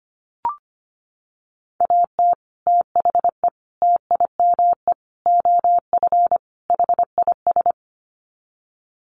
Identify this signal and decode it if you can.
Morse code, heard as a keyed tone, about 25 wpm, 700 Hz. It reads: AT THE TIME OF HIS
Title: Sets of 5 Words Encore2 25wpm